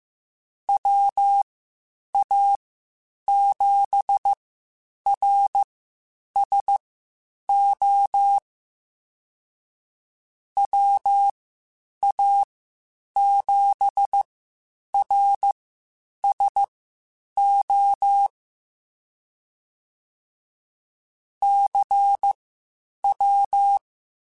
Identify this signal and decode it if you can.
Morse code, heard as a keyed tone, about 15 wpm, spaced as for 8 wpm, 800 Hz. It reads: WA7RSO WA7RSO CW
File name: Misc / Misc-mp3s / QST QST @13